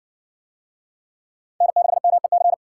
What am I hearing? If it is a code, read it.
Morse code, heard as a keyed tone, about 58 wpm, 700 Hz. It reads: N6MEC